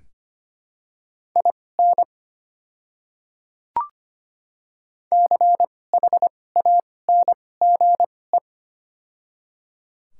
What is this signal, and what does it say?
Morse code, heard as a keyed tone, about 25 wpm, spaced as for 18 wpm, 700 Hz. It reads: IN CHANGE
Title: 2x Character Spacing - Top 500 Words - 25wpm